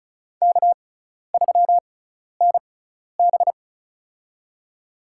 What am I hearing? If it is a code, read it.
Morse code, heard as a keyed tone, about 36 wpm, spaced as for 12 wpm, 700 Hz. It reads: K3NB
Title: USA Call Signs - 4 Characters - 6x Character Spacing - 35wpm